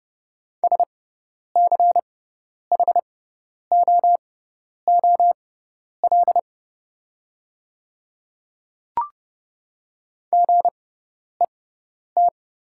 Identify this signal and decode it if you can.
Morse code, heard as a keyed tone, about 31 wpm, spaced as for 10 wpm, 700 Hz. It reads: SCHOOL GET
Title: 6x Character Spacing - Top 300 Words - 30wpm